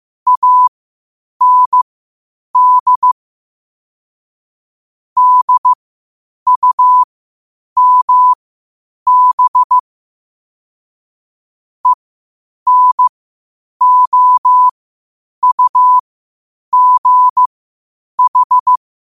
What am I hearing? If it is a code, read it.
Morse code, heard as a keyed tone, about 15 wpm, spaced as for 8 wpm, 1000 Hz. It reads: AND DUMB ENOUGH
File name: qotdfiles / 8wpm